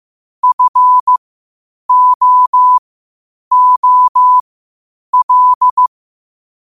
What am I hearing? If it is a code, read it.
Morse code, heard as a keyed tone, about 15 wpm, spaced as for 8 wpm, 1000 Hz. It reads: FOOL